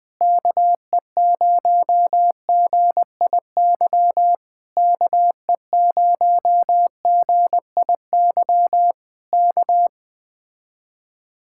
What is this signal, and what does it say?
Morse code, heard as a keyed tone, about 20 wpm, 700 Hz. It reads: KE0GIY KE0GIY K
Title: Speed-Racing Common QSO Phrases - 17wpm